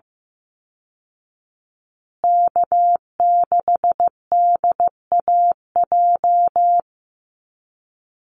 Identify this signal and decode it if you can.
Morse code, heard as a keyed tone, about 15 wpm, 700 Hz. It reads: K6DAJ
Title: Call Signs Repeated 2x - 15wpm